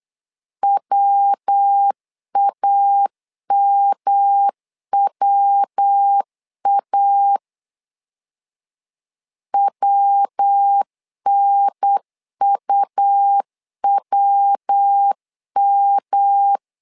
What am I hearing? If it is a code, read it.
Morse code, heard as a keyed tone, about 8 wpm, 800 Hz. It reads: WAMWA WNUWM
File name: LEZIONI / 3 MWRO / 3 MW 8wpm 2501